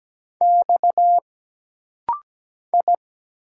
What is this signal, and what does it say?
Morse code, heard as a keyed tone, about 17 wpm, 700 Hz. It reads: X I
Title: Single Letters - Rapid-Fire 17wpm